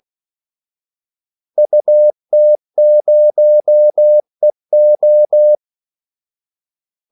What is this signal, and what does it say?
Morse code, heard as a keyed tone, about 16 wpm, 600 Hz. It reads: UT0EO
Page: Kaldesignaler 16 wpm